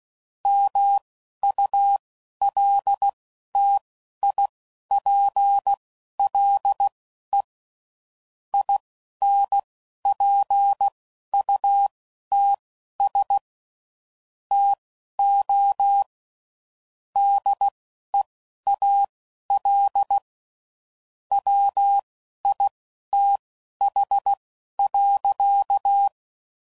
Morse code, heard as a keyed tone, about 16 wpm, spaced as for 11 wpm, 800 Hz. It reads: MULTIPLE INPUTS TO DEAL WITH.
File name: CW Code Groups @10WPM / Technical Subjects / AND Gate